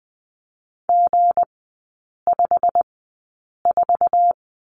Morse code, heard as a keyed tone, about 20 wpm, spaced as for 8 wpm, 690 Hz. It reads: G54
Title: 159 Lesson 26e - Letter g - Three Characters 20wpm